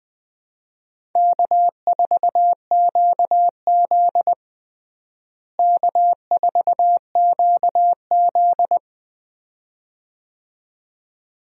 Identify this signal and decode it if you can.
Morse code, heard as a keyed tone, about 20 wpm, 700 Hz. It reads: K4QZ K4QZ
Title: USA Call Signs - 4 Characters - Repeated 2x - 20wpm